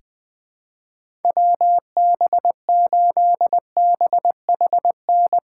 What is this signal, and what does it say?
Morse code, heard as a keyed tone, about 21 wpm, 700 Hz. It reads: WB8BHN
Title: 2023 SOTA Chaser Call Signs - 20wpm